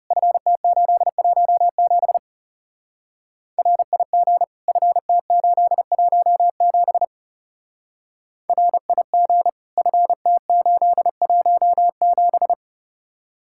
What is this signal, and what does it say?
Morse code, heard as a keyed tone, about 34 wpm, 700 Hz. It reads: FT817 RIG FT817 RIG FT817
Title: Speed-Racing Common QSO Phrases - 30wpm